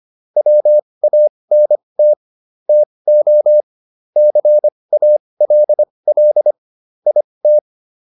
Morse code, heard as a keyed tone, about 25 wpm, spaced as for 20 wpm, 600 Hz. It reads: WANT TO CALL IT